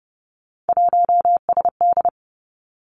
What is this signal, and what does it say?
Morse code, heard as a keyed tone, about 30 wpm, 700 Hz. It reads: 1SD
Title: Three Challenging Letter-Number Combinations 30wpm